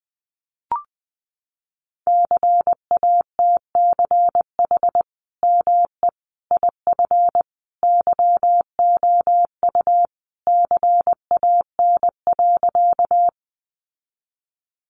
Morse code, heard as a keyed tone, about 20 wpm, 700 Hz. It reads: CATCH ME IF YOU CAN.
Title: Sentences from Top 1500 Words 20wpm